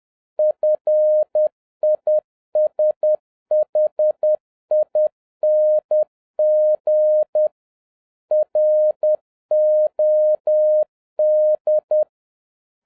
Morse code, heard as a keyed tone, about 10 wpm, 610 Hz. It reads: FISHING ROD
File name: 2011-07-mp3 / n41 EXCITING QRP CW OPERATION 10wpm